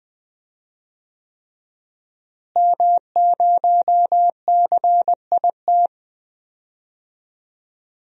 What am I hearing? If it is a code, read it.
Morse code, heard as a keyed tone, about 20 wpm, 700 Hz. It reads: M0CIT
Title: DXCC-100 Call Signs - 5 Characters - Repeated 3x - 20wpm